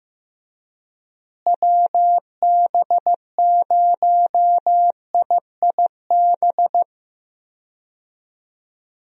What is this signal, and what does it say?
Morse code, heard as a keyed tone, about 15 wpm, 700 Hz. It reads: WB0IIB
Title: USA Call Signs - 6 Characters - Repeated 3x - 15wpm